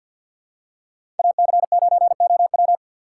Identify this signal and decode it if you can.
Morse code, heard as a keyed tone, about 47 wpm, 690 Hz. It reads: AC9OW
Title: USA Call Signs - 6x Character Spacing - 50wpm